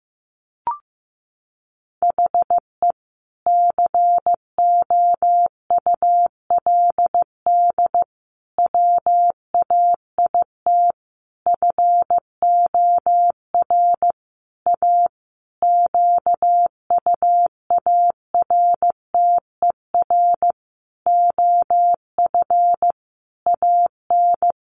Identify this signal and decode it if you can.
Morse code, heard as a keyed tone, about 15 wpm, 700 Hz. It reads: HE COULD WAIT FOR A QUARTER OF AN
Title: Sentences from Top 1300 Words 15wpm